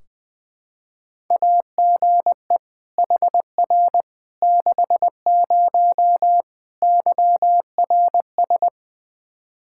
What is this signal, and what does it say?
Morse code, heard as a keyed tone, about 20 wpm, 700 Hz. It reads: AGE HR 60 YRS
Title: Practice QSO - Exchange 3 - More Info - 20wpm